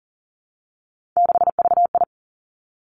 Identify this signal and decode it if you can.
Morse code, heard as a keyed tone, about 40 wpm, 700 Hz. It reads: 6VI
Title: Three Challenging Letter-Number Combinations 40wpm